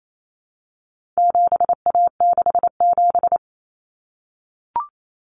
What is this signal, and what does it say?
Morse code, heard as a keyed tone, about 28 wpm, 700 Hz. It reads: 7A67
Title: Common Cut Numbers - 4-Digit 28wpm